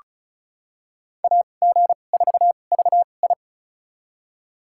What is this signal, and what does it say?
Morse code, heard as a keyed tone, about 36 wpm, spaced as for 25 wpm, 700 Hz. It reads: AG4VI
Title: USA Call Signs - 5 Characters - 2x Character Spacing - 35wpm